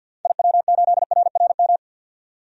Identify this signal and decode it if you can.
Morse code, heard as a keyed tone, about 50 wpm, 700 Hz. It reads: IW8RRM
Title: DXCC-100 Call Signs - 6 Characters - 50wpm